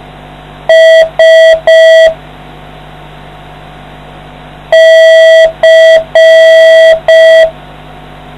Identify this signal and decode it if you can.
Morse code, heard as a keyed tone, about 5 wpm, 650 Hz. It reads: S C